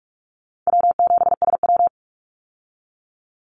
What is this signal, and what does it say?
Morse code, heard as a keyed tone, about 44 wpm, 700 Hz. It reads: W7SW
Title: Easy Call Signs Repeated 2x - 45wpm